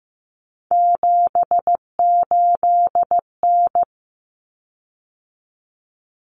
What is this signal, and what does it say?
Morse code, heard as a keyed tone, about 15 wpm, 700 Hz. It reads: 78N
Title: Common Cut Numbers - 3-Digit 15wpm